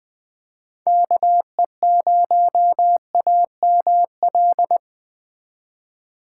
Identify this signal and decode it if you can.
Morse code, heard as a keyed tone, about 21 wpm, 700 Hz. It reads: KE0AML